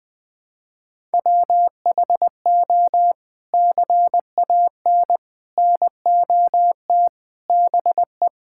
Morse code, heard as a keyed tone, about 20 wpm, 700 Hz. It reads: WHO CAN NOT BE
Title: Sets of 8 Words - 20wpm